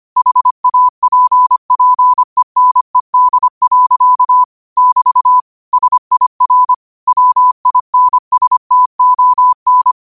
Morse code, heard as a keyed tone, about 25 wpm, 1000 Hz. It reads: HAPPENED. <BT> SIR WINSTON
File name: qotdfiles / 25wpm